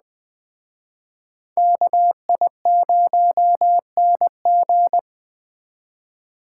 Morse code, heard as a keyed tone, about 20 wpm, 700 Hz. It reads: KI0NG
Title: Speed Racing - USA 2x2 Call Signs 17wpm